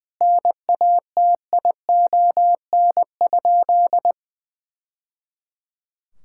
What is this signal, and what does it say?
Morse code, heard as a keyed tone, about 20 wpm, 700 Hz. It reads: NATION?
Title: Sentences from Top 1000 Words - Review Part 6 - 20wpm